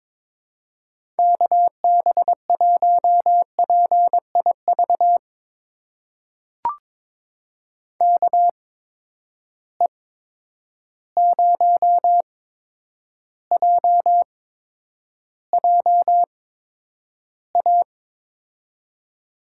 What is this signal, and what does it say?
Morse code, heard as a keyed tone, about 21 wpm, spaced as for 9 wpm, 700 Hz. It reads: KB1PIV KE0JJA
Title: USA Call Signs - 6 Characters - 8x Character Spacing - 22wpm